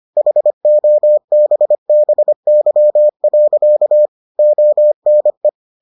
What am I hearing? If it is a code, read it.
Morse code, heard as a keyed tone, about 25 wpm, 600 Hz. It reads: HOBBY. ONE